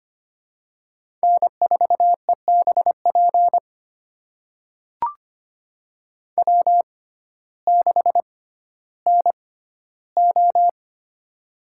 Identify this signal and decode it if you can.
Morse code, heard as a keyed tone, about 25 wpm, spaced as for 14 wpm, 700 Hz. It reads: N4EBP W6NO
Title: USA Call Signs - 6x Character Spacing - 25wpm